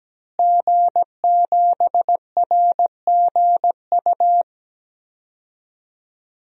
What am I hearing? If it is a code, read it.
Morse code, heard as a keyed tone, about 17 wpm, 700 Hz. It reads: G7RGU
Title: DXCC-100 Call Signs - 5 Characters - Repeated 3x - 17wpm